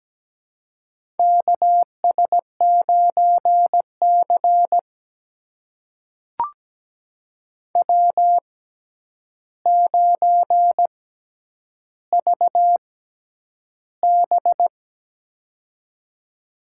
Morse code, heard as a keyed tone, about 17 wpm, spaced as for 9 wpm, 700 Hz. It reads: KS9C W9VB
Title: USA Call Signs - 4 Characters - 6x Character Spacing - 17wpm